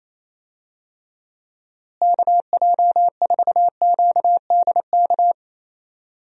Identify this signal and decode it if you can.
Morse code, heard as a keyed tone, about 28 wpm, 700 Hz. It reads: KJ4QDK
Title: USA Call Signs - 6 Characters - Repeated 3x - 28wpm